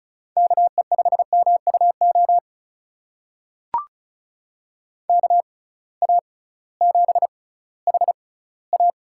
Morse code, heard as a keyed tone, about 34 wpm, spaced as for 19 wpm, 700 Hz. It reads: KE5MUO KA7HA
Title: USA Call Signs - 6 Characters - 6x Character Spacing - 35wpm